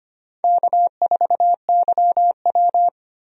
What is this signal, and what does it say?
Morse code, heard as a keyed tone, about 25 wpm, 700 Hz. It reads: K4YW
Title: USA Call Signs - 4 Characters - 25wpm